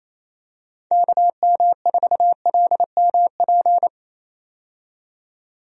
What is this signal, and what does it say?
Morse code, heard as a keyed tone, about 28 wpm, 700 Hz. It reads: KM4LMP
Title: USA Call Signs - 6 Characters - 28wpm